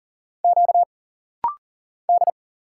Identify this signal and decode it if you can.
Morse code, heard as a keyed tone, about 40 wpm, 700 Hz. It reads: Q D
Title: Single Letters - Rapid-Fire 40wpm